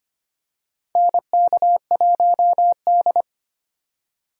Morse code, heard as a keyed tone, about 25 wpm, 700 Hz. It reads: NK1D
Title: USA Call Signs - 4 Characters - 4x Character Spacing - 25wpm